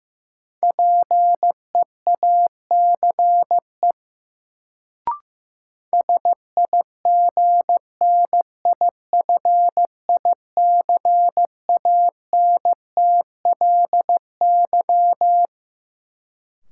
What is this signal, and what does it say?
Morse code, heard as a keyed tone, about 15 wpm, 700 Hz. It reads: PEACE SIGNIFICANTLY
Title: Sentences from Top 1500 Words - New Word Drill Alt - 15wpm